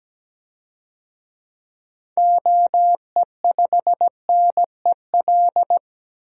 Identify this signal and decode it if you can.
Morse code, heard as a keyed tone, about 17 wpm, 700 Hz. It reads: OE5NEL
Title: DXCC-100 Call Signs - 6 Characters - 17wpm